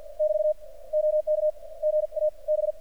Morse code, heard as a keyed tone, about 50 wpm, 600 Hz. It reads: <BT> OK GA B
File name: archived / CRSnet